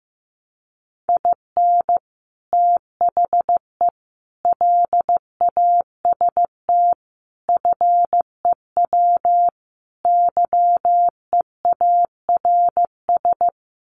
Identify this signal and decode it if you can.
Morse code, heard as a keyed tone, about 15 wpm, 700 Hz. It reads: IN THE LAST FEW YEARS
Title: Sets of 5 Words 15wpm